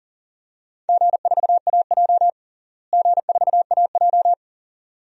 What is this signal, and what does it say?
Morse code, heard as a keyed tone, about 40 wpm, 700 Hz. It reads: G4AJ G4AJ
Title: DXCC-100 Call Signs - 4 Characters - Repeated 2x - 40wpm